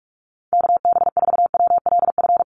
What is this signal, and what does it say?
Morse code, heard as a keyed tone, about 44 wpm, 700 Hz. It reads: KB4WLF
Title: Call Signs Repeated 3x - 45wpm